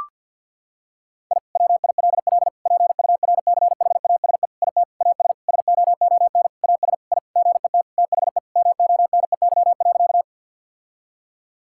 Morse code, heard as a keyed tone, about 52 wpm, 700 Hz. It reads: I WILL PURCHASE IT AS SOON AS I GET THE MONEY.
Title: Sentences from Top 1000 Words - Review Part 5 - 50wpm